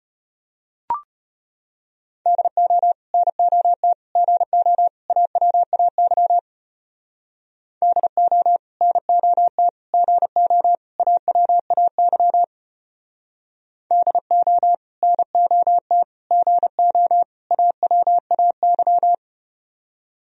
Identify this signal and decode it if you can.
Morse code, heard as a keyed tone, about 33 wpm, 690 Hz. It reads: DO NOT GO AWAY DO NOT GO AWAY DO NOT GO AWAY